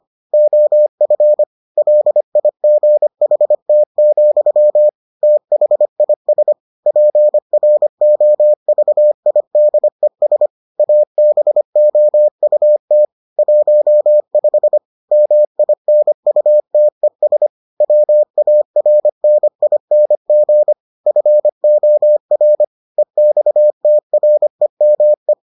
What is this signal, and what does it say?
Morse code, heard as a keyed tone, about 25 wpm, 600 Hz. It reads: OF LIGHT, THIS PROVIDES ABOUT 15 MINUTES WARNING FOR EXTREME